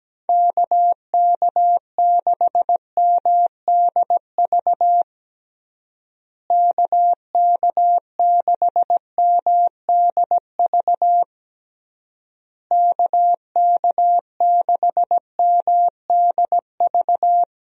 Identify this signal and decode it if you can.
Morse code, heard as a keyed tone, about 17 wpm, 700 Hz. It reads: KK6MDV KK6MDV KK6MDV
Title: USA Call Signs - 6 Characters - Repeated 3x - 17wpm